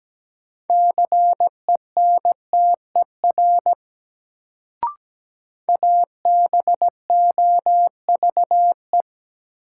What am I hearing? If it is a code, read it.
Morse code, heard as a keyed tone, about 17 wpm, 700 Hz. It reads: CENTER ABOVE
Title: Sentences from Top 1000 Words - Review Part 2 - New Word Drill - 17wpm